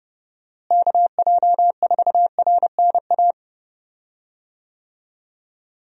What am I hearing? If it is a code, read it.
Morse code, heard as a keyed tone, about 30 wpm, 700 Hz. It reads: KJ4RNA
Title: USA Call Signs - 6 Characters - Repeated 2x - 30wpm